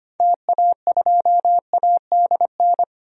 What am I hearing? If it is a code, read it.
Morse code, heard as a keyed tone, about 25 wpm, 700 Hz. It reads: TA2ADN
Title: DXCC-100 Call Signs - 6 Characters - 25wpm